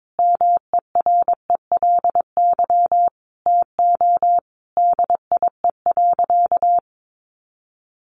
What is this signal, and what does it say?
Morse code, heard as a keyed tone, about 22 wpm, 700 Hz. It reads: MERELY TO DIE.
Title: Sentences from Top 1400 Words 22wpm